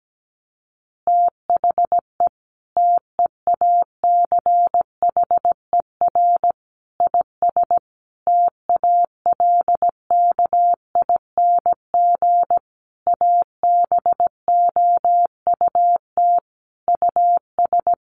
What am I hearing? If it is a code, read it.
Morse code, heard as a keyed tone, about 17 wpm, 700 Hz. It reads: THE TEACHER IS TALKING ABOUT US
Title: Easy Phrases 17wpm